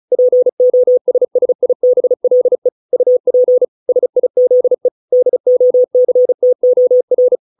Morse code, heard as a keyed tone, about 34 wpm, 500 Hz. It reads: POSSIBLE UP SIZE DOCTOR